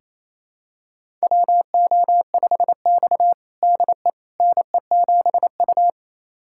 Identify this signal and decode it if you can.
Morse code, heard as a keyed tone, about 28 wpm, 700 Hz. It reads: WO5X DE NE7U
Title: USA Callsign DE Callsign - 4 Characters - 28wpm